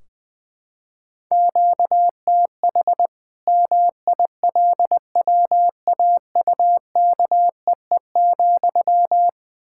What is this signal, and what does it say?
Morse code, heard as a keyed tone, about 20 wpm, 700 Hz. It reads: QTH MILWAUKEE,
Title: Practice QSO - Exchange 1 - Beginning - 20wpm